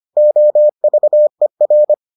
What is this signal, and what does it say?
Morse code, heard as a keyed tone, about 25 wpm, 600 Hz. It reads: OVER